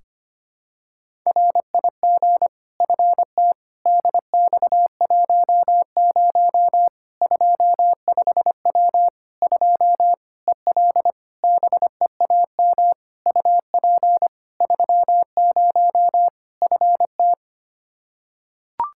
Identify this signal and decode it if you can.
Morse code, heard as a keyed tone, about 25 wpm, 700 Hz. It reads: RIG FT DX10 25W 2 EL BEAM UP 30 FT
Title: Practice QSO - Exchange 2 - Info - 25wpm